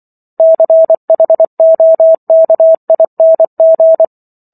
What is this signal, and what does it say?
Morse code, heard as a keyed tone, about 24 wpm, 650 Hz. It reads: CHOKING